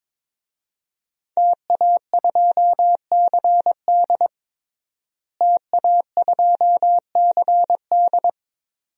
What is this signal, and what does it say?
Morse code, heard as a keyed tone, about 22 wpm, 700 Hz. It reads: TA2CD TA2CD
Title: DXCC-100 Call Signs - 5 Characters - Repeated 2x - 22wpm